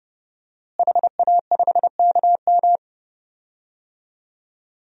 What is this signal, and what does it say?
Morse code, heard as a keyed tone, about 32 wpm, 700 Hz. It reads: HA5KM
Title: DXCC-100 Call Signs - 5 Characters - Repeated 2x - 30wpm